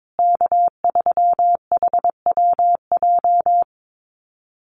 Morse code, heard as a keyed tone, about 22 wpm, 700 Hz. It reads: K3HWJ
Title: USA Call Signs Repeated 3x 22wpm